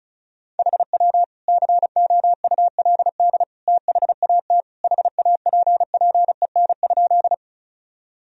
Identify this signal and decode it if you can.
Morse code, heard as a keyed tone, about 36 wpm, 700 Hz. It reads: HW COULD THAT HAPPEN?
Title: Sentences from Top 400 Words and Abbreviations 35wpm